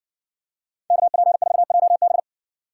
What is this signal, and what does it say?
Morse code, heard as a keyed tone, about 54 wpm, 700 Hz. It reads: KP4JB